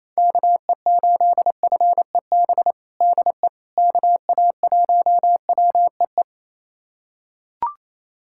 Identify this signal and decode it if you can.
Morse code, heard as a keyed tone, about 28 wpm, 700 Hz. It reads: KE8FEB DE KA1WEE
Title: USA Callsign DE Callsign - 6 Characters - 28wpm